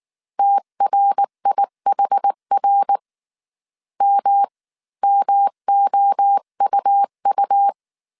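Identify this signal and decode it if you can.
Morse code, heard as a keyed tone, about 19 wpm, spaced as for 17 wpm, 800 Hz. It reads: TRIHR MMOUU
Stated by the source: Morse code dictation at 18 wpm from the IW3SGT QRS Morse code training course, five-character groups of 300 characters.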